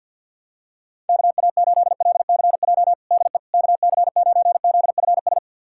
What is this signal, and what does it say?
Morse code, heard as a keyed tone, about 53 wpm, 700 Hz. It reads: KA8LCJ DE KC9ZFS
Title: USA Callsign DE Callsign - 6 Characters - 50wpm